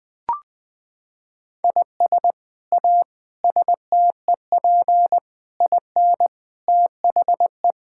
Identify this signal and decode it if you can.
Morse code, heard as a keyed tone, about 20 wpm, 700 Hz. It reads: IS A STEP IN THE
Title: Sets of 8 Words - 20wpm